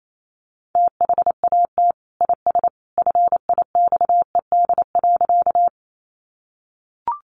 Morse code, heard as a keyed tone, about 28 wpm, 700 Hz. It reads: THAT IS FIXED.
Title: Sentences from Top 1100 Words 28wpm